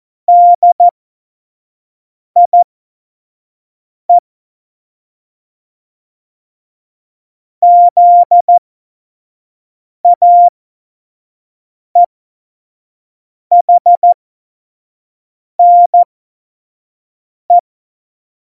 Morse code, heard as a keyed tone, about 14 wpm, spaced as for 5 wpm, 700 Hz. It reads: DIE ZAEHNE
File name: Morsekurs / Witze / Witz 11. September 2018, 25 F70